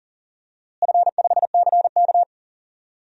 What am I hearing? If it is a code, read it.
Morse code, heard as a keyed tone, about 40 wpm, 700 Hz. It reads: F5CK
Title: DXCC-100 Call Signs - 4 Characters - Repeated 3x - 40wpm